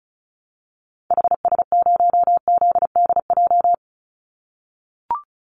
Morse code, heard as a keyed tone, about 34 wpm, 700 Hz. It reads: HS0ZDJ
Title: Call Signs - 35wpm